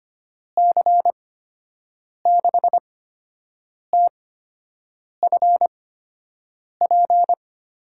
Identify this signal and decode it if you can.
Morse code, heard as a keyed tone, about 25 wpm, spaced as for 7 wpm, 700 Hz. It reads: C6TFP
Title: USA Call Signs - 6 Characters - 8x Character Spacing - 25wpm